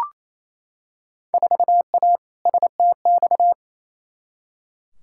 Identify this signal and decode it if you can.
Morse code, heard as a keyed tone, about 27 wpm, 700 Hz. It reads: 4A STX
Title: Field Day Exchanges Encore - 28wpm